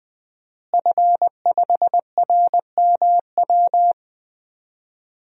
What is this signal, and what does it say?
Morse code, heard as a keyed tone, about 21 wpm, 700 Hz. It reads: F5RMW
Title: DXCC-100 Call Signs - 5 Characters - Repeated 3x - 20wpm